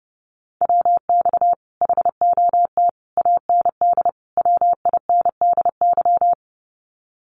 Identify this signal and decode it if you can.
Morse code, heard as a keyed tone, about 30 wpm, 700 Hz. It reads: WX HOT AND WINDY
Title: Common QSO Phrases Encore 30wpm